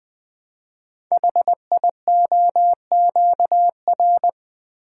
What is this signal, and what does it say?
Morse code, heard as a keyed tone, about 20 wpm, 700 Hz. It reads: HIOQR
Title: Code Groups - 5 Letter:Numbers - 20wpm